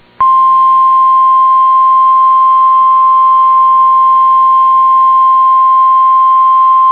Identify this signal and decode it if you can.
Emergency Alert System (SAME) tones (the 1050 Hz attention tone). It is heard from 0.8 to 6.4 s.